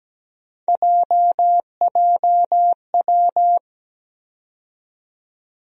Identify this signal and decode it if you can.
Morse code, heard as a keyed tone, about 17 wpm, 700 Hz. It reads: JJW
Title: USA Call Sign Postfixes - 3 Letters 17wpm